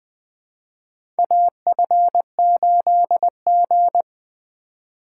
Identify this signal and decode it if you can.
Morse code, heard as a keyed tone, about 20 wpm, 700 Hz. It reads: AF8G
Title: USA Call Signs - 4 Characters - 2x Character Spacing - 20wpm